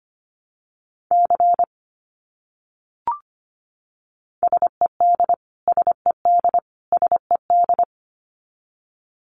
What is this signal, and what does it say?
Morse code, heard as a keyed tone, about 25 wpm, 700 Hz. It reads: C SED SED SED
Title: Abbreviations Repeated 3x - 25wpm